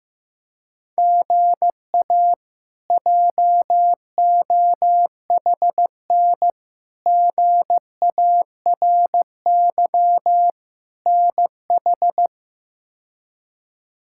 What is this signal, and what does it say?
Morse code, heard as a keyed tone, about 15 wpm, 700 Hz. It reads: GA JOHN GARY NH
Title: K1USN SST QSO Practice - 15wpm